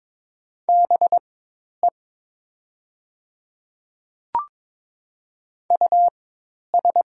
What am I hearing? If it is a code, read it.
Morse code, heard as a keyed tone, about 22 wpm, spaced as for 10 wpm, 700 Hz. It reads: BE US
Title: 4x Character Spacing - Top 200 Words - 22wpm